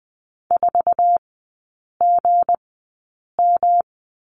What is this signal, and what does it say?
Morse code, heard as a keyed tone, about 20 wpm, spaced as for 8 wpm, 700 Hz. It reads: 4GM
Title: 166 Lesson 27e - Number 7 - Three Characters 20wpm